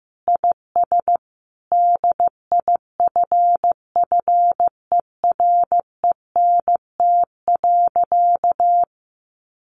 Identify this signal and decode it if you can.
Morse code, heard as a keyed tone, about 15 wpm, 700 Hz. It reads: IS DIFFERENT.
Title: Sentences from Top 600 Words 15wpm